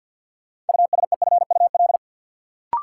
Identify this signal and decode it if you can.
Morse code, heard as a keyed tone, about 52 wpm, 700 Hz. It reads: USEFUL
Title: Sentences from Top 700 Words - New Word Drill - 50wpm